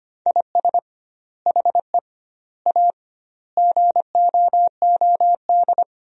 Morse code, heard as a keyed tone, about 25 wpm, 700 Hz. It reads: IS HE A GOOD
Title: 2x Word-Spacing - Sentences from Top 200 Words - 25wpm